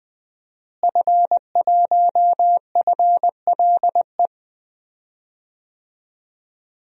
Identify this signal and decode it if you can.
Morse code, heard as a keyed tone, about 20 wpm, 700 Hz. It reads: F1FLE
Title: DXCC-100 Call Signs - 5 Characters - Repeated 2x - 20wpm